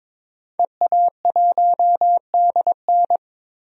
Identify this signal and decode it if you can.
Morse code, heard as a keyed tone, about 22 wpm, 700 Hz. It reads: EA1DN
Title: DXCC-100 Call Signs - 5 Characters - 22wpm